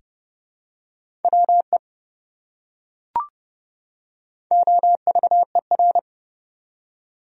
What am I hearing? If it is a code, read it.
Morse code, heard as a keyed tone, about 30 wpm, 700 Hz. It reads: WE OVER
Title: Top 100 Words 30wpm